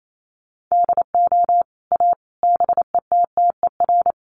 Morse code, heard as a keyed tone, about 28 wpm, 700 Hz. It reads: DO A BETTER
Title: Sets of 5 Words Encore2 28wpm